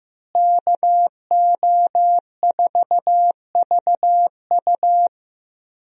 Morse code, heard as a keyed tone, about 15 wpm, 700 Hz. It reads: KO4VU
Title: USA Call Signs - 5 Characters - Repeated 2x - 15wpm